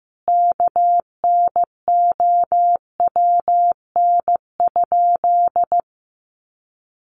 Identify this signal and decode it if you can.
Morse code, heard as a keyed tone, about 15 wpm, 700 Hz. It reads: KNOWN?
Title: Sentences from Top 600 Words 15wpm